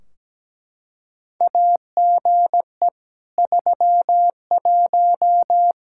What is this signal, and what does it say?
Morse code, heard as a keyed tone, about 17 wpm, 700 Hz. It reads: AGE 31
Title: Practice QSO - Exchange 3 - More Info - 17wpm